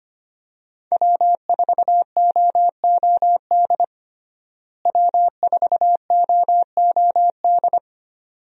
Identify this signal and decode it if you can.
Morse code, heard as a keyed tone, about 25 wpm, 700 Hz. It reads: W4OOD W4OOD
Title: USA Call Signs - 5 Characters - Repeated 2x - 25wpm